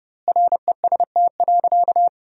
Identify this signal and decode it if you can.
Morse code, heard as a keyed tone, about 32 wpm, 700 Hz. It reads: REST.
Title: 2x Word-Spacing - Sentences from Top 500 Words - 30wpm